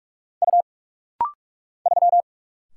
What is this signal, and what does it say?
Morse code, heard as a keyed tone, about 46 wpm, 700 Hz. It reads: U 3
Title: Single Letter-Number Alt - Rapid-Fire 45wpm